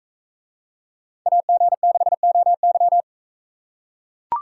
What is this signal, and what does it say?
Morse code, heard as a keyed tone, about 42 wpm, 700 Hz. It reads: AG6OY